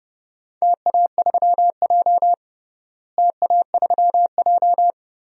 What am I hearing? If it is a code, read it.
Morse code, heard as a keyed tone, about 29 wpm, 700 Hz. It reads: TA3J TA3J
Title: DXCC-100 Call Signs - 4 Characters - Repeated 3x - 30wpm